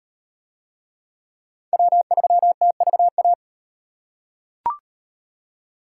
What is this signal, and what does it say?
Morse code, heard as a keyed tone, about 35 wpm, 700 Hz. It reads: W3TVA